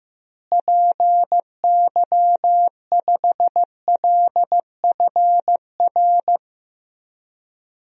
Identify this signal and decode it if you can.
Morse code, heard as a keyed tone, about 15 wpm, 700 Hz. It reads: PY5LFR
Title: DXCC-100 Call Signs - 6 Characters - Repeated 2x - 15wpm